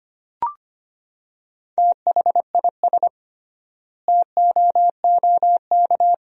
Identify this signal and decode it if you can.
Morse code, heard as a keyed tone, about 25 wpm, 700 Hz. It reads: THIS TOOK
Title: 3x Word-Spacing - Sentences from Top 300 Words - 25wpm